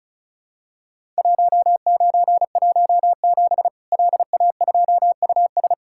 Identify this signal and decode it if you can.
Morse code, heard as a keyed tone, about 36 wpm, 700 Hz. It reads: 1917 LA2US
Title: Worked All Europe DX Contest - QTC Exchange - 35wpm